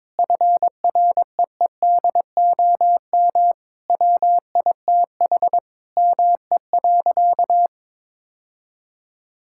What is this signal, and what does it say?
Morse code, heard as a keyed tone, about 21 wpm, 700 Hz. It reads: FREEDOM WITH ME.
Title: Sentences from Top 1000 Words - Review Part 5 - 22wpm